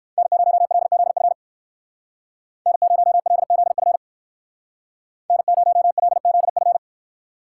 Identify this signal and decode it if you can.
Morse code, heard as a keyed tone, about 59 wpm, 690 Hz. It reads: N0LZF N0LZF N0LZF